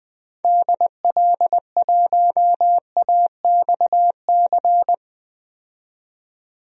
Morse code, heard as a keyed tone, about 20 wpm, 700 Hz. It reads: DL1AXC
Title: DXCC-100 Call Signs - 6 Characters - 20wpm